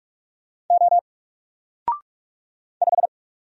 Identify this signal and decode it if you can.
Morse code, heard as a keyed tone, about 44 wpm, 700 Hz. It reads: O 5
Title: Single Character (Letter:Number:Punctuation:Prosign) 45wpm